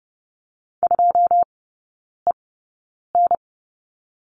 Morse code, heard as a keyed tone, about 30 wpm, spaced as for 9 wpm, 700 Hz. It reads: 2EN
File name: course / 30:10wpm / 087 Lesson 15e - Number 5 - Three Characters 30wpm